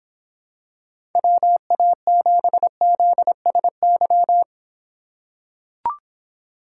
Morse code, heard as a keyed tone, about 25 wpm, 700 Hz. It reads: WA7ZSY